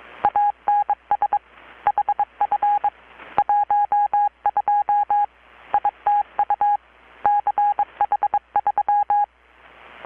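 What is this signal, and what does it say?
Morse code, heard as a keyed tone, about 22 wpm, 820 Hz. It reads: ANS HF 12 ITU CH3
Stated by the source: Morse from KPH's Night of Nights XX maritime radiotelegraphy event, received in Chiapas, Mexico about 3,800 km away on a Kenwood R-600 with a grounded 10 m long-wire antenna.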